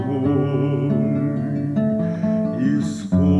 There is no code.